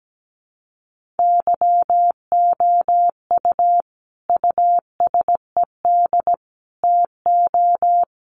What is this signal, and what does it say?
Morse code, heard as a keyed tone, about 17 wpm, 700 Hz. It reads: YOU USED TO
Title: Sentences from Top 600 Words 17wpm